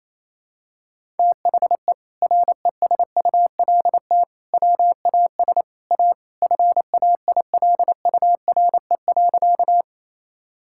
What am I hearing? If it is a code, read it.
Morse code, heard as a keyed tone, about 28 wpm, 700 Hz. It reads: THE RESULT WAS A FAILURE.
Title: Sentences from Top 1000 Words - Review Part 6 - 28wpm